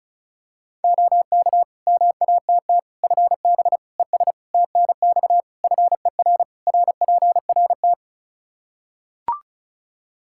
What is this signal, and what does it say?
Morse code, heard as a keyed tone, about 36 wpm, 700 Hz. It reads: OK MATT FB ES TNX FER RPRT
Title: Practice QSO - Exchange 2 - Info - 35wpm